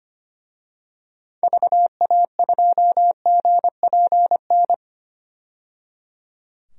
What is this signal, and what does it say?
Morse code, heard as a keyed tone, about 25 wpm, 700 Hz. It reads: VA2GPN
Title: DXCC-100 Call Signs - 6 Characters - 25wpm